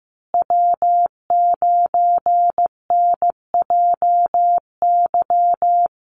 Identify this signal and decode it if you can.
Morse code, heard as a keyed tone, about 15 wpm, 700 Hz. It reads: W9NJY
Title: Call Signs Repeated 3x - 15wpm